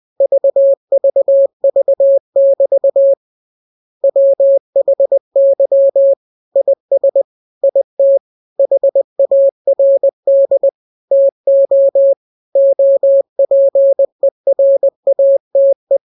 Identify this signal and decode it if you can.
Morse code, heard as a keyed tone, about 20 wpm, 550 Hz. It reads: VVV<BT> WHY IS IT HARD TO OPERATE